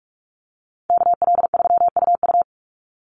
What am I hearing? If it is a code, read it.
Morse code, heard as a keyed tone, about 44 wpm, 700 Hz. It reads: KL3UU
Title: USA Call Signs 45wpm